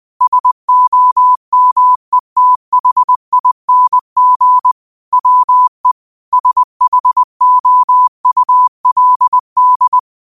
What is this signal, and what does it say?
Morse code, heard as a keyed tone, about 20 wpm, 1000 Hz. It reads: SOMETHING WE SHOULD
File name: qotdfiles / 20wpm